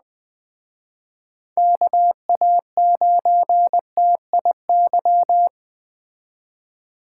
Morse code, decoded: KA9TIY

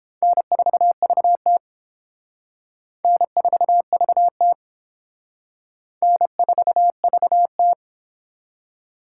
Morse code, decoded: N4VT N4VT N4VT

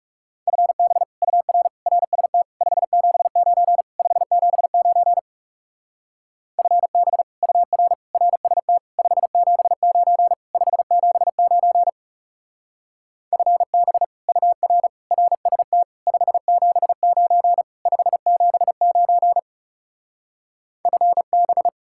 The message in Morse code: FB UR RST 579 579 FB UR RST 579 579 FB UR RST 579 579 FB

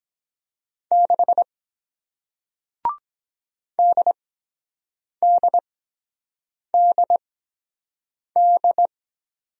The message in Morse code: 6 DDDD

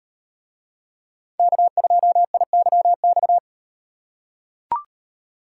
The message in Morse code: K2IYX

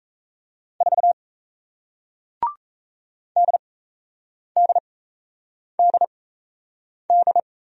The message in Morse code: 4 DDDD

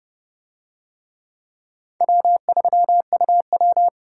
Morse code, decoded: W3UW

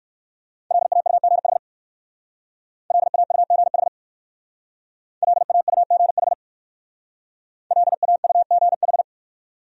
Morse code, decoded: LAUGH LAUGH LAUGH LAUGH

